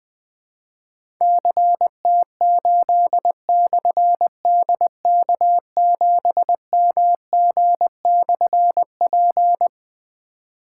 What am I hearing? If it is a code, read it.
Morse code, heard as a keyed tone, about 20 wpm, 700 Hz. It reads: CT8/DK7MG/P
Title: Difficult Call Signs 20wpm